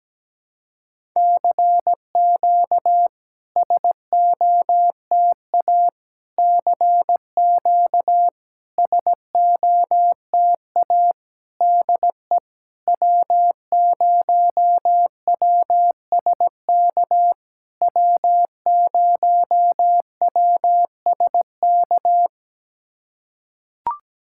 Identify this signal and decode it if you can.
Morse code, heard as a keyed tone, about 17 wpm, 660 Hz. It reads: CQ SOTA CQ SOTA DE W0WSK W0WSK
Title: SOTA QSO Practice 17wpm